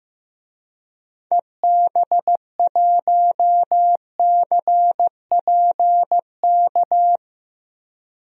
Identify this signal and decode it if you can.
Morse code, heard as a keyed tone, about 16 wpm, 700 Hz. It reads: EB1CPK
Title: DXCC-100 Call Signs - 6 Characters - Repeated 3x - 15wpm